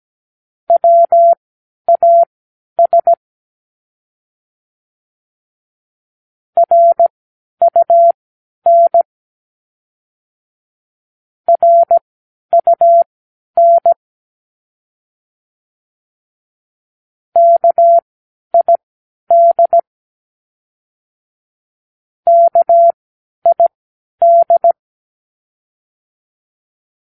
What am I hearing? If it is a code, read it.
Morse code, heard as a keyed tone, about 17 wpm, spaced as for 10 wpm, 680 Hz. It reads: WAS RUN RUN KID KID
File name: k7qo / Practicewords